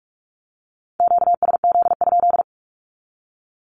Morse code, heard as a keyed tone, about 44 wpm, 700 Hz. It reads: QSZ?
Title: Q-Codes 45wpm